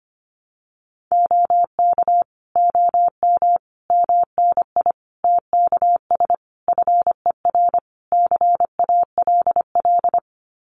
Morse code, heard as a keyed tone, about 25 wpm, 700 Hz. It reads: OK OM MNI TKS FER CALL
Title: Common QSO Phrases 25wpm